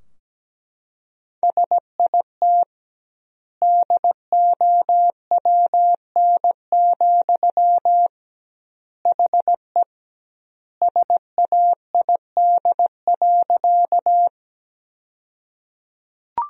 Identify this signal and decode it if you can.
Morse code, heard as a keyed tone, about 17 wpm, 700 Hz. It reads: SIT DOWN, HE SAID.
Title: 2x Word-Spacing - Sentences from Top 500 Words - 17wpm